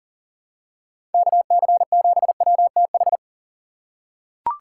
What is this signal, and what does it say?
Morse code, heard as a keyed tone, about 40 wpm, 700 Hz. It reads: KC7WTH
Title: USA Call Signs - 6 Characters - 6x Character Spacing - 40wpm